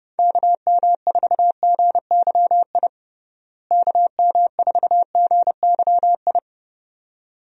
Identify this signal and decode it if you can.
Morse code, heard as a keyed tone, about 32 wpm, 700 Hz. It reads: KM4GYI KM4GYI
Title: USA Call Signs - 6 Characters - Repeated 2x - 30wpm